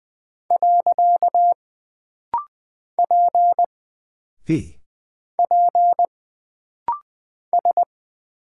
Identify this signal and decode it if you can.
Morse code, heard as a keyed tone, about 20 wpm, 700 Hz. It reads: . P P S
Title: Single Character (Letter:Number:Punctuation:Prosign) - Rapid-Fire - 20wpm